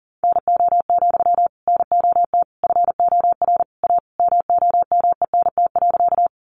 Morse code, heard as a keyed tone, about 40 wpm, 700 Hz. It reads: NO, NOT FOR A MOMENT.
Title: Sentences from Top 1000 Words 40wpm